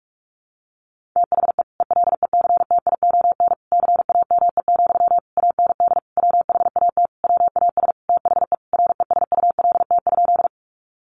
Morse code, heard as a keyed tone, about 44 wpm, 700 Hz. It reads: THE ELECTION CAME, AND WHAT WAS THE RESULT?
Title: Sentences from Top 1500 Words 45wpm